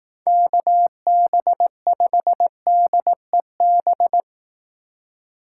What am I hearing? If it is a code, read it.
Morse code, heard as a keyed tone, about 18 wpm, 700 Hz. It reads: KB5DEB
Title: Speed Racing - USA Call Signs 15wpm